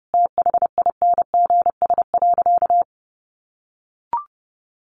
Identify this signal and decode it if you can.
Morse code, heard as a keyed tone, about 30 wpm, 700 Hz. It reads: THINGS.
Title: A Sound Of Thunder 30wpm